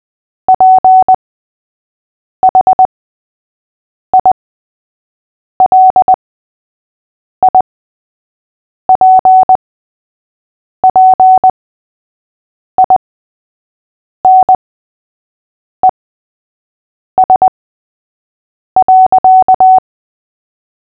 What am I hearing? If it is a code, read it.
Morse code, heard as a keyed tone, about 20 wpm, spaced as for 6 wpm, 750 Hz. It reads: PHILIPPINES.
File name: cw / fists / Short-Story-3-6wpm